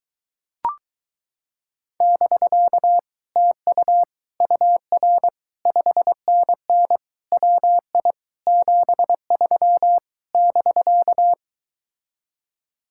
Morse code, heard as a keyed tone, about 24 wpm, 700 Hz. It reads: <BK> TU UR 5NN WI 73 <BK>